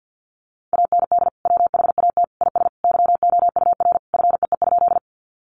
Morse code, heard as a keyed tone, about 50 wpm, 700 Hz. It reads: AND WHAT IS YOUR FEE?
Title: Sentences from Top 1500 Words 50wpm